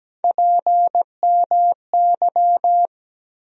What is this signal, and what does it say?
Morse code, heard as a keyed tone, about 17 wpm, 700 Hz. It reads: PMY
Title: USA Call Sign Postfixes - 3 Letters 17wpm